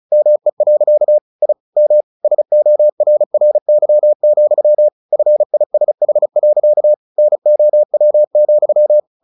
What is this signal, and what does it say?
Morse code, heard as a keyed tone, about 34 wpm, 600 Hz. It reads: ME. I M SORRY, FISH. NOW,
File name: the old man and the sea 2 35wpm